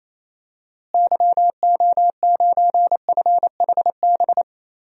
Morse code, heard as a keyed tone, about 28 wpm, 700 Hz. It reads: YO9FHB
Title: DXCC-100 Call Signs - 6 Characters - Repeated 2x - 28wpm